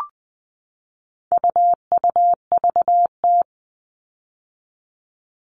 Morse code, heard as a keyed tone, about 20 wpm, 700 Hz. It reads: UUVT
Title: All Cut Numbers - 4-Digit 20wpm